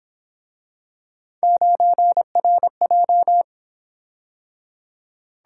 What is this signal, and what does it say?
Morse code, heard as a keyed tone, about 26 wpm, 700 Hz. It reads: 9RJ
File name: Speed-Racing - Three Letter-Number Combinations - 22wpm